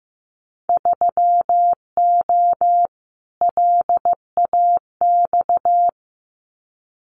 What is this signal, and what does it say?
Morse code, heard as a keyed tone, about 15 wpm, 700 Hz. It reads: 3O LAX
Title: Winter Field Day Exchanges Encore 15wpm